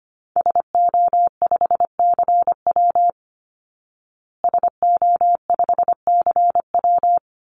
Morse code, heard as a keyed tone, about 25 wpm, 700 Hz. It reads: SO5CW SO5CW
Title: Call Signs Repeated 3x - 25wpm